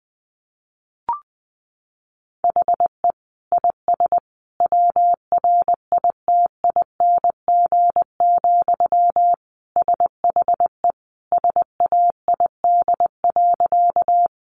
Morse code, heard as a keyed tone, about 20 wpm, 700 Hz. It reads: HE IS WRITING, SHE SAID.
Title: Sentences from Top 800 Words 20wpm